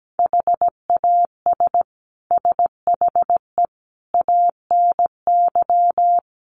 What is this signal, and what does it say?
Morse code, heard as a keyed tone, about 17 wpm, 700 Hz. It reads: HAS SHE ANY
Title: Sentences from Top 400 Words 17wpm